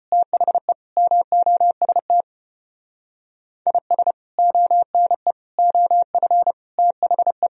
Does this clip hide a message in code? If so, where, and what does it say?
Yes: Morse code from 0.1 to 7.5 s, heard as a keyed tone, about 32 wpm, 700 Hz. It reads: THE MOST IS ONE OF THE